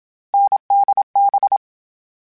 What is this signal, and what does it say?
Morse code, heard as a keyed tone, about 27 wpm, 800 Hz. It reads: NDB
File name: Difficult Combos / Difficult Combos-mp3s / awj1 ndb6 gz7 @25